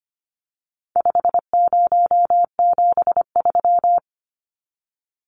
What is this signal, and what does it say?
Morse code, heard as a keyed tone, about 25 wpm, 700 Hz. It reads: 5073
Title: Numbers - 4-Digit 25wpm